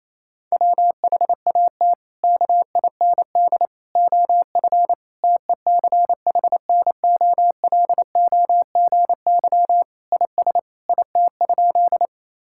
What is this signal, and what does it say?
Morse code, heard as a keyed tone, about 28 wpm, 700 Hz. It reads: WHAT KIND OF TECHNOLOGY IS IT?
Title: Sentences from Top 1000 Words - Review Part 5 - 28wpm